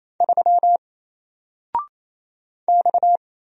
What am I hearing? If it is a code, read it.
Morse code, heard as a keyed tone, about 27 wpm, 700 Hz. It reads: 3 X
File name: Single-Letter-Number-28wpm